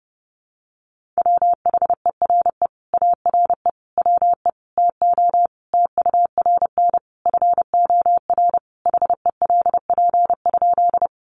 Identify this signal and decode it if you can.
Morse code, heard as a keyed tone, about 30 wpm, 700 Hz. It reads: WHERE ARE WE TO TURN FOR HELP?
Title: Sentences from Top 200 Words 30wpm